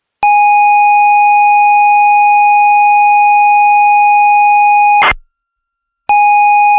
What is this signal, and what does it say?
ZX Spectrum tape loading.